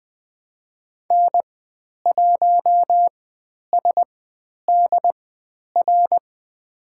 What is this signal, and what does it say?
Morse code, heard as a keyed tone, about 20 wpm, spaced as for 10 wpm, 700 Hz. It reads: N1SDR
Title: 109 Lesson 18g - Letter m - Call Signs 20wpm